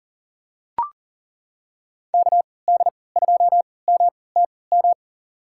Morse code, heard as a keyed tone, about 40 wpm, spaced as for 23 wpm, 700 Hz. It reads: KD2MTM
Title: USA Call Signs - 6 Characters - 3x Character Spacing - 40wpm